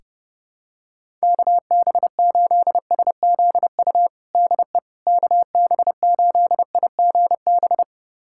Morse code, heard as a keyed tone, about 32 wpm, 700 Hz. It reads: KB8SZU DE KB8IGB
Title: USA Callsign DE Callsign - 6 Characters - 30wpm